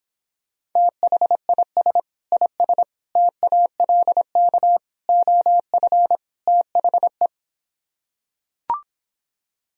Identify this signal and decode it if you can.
Morse code, heard as a keyed tone, about 26 wpm, 700 Hz. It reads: THIS IS TALK OF THE